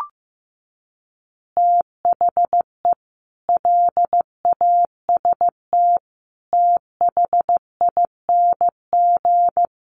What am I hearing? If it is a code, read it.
Morse code, heard as a keyed tone, about 15 wpm, 700 Hz. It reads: THE LAST THING
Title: Easy Phrases 15wpm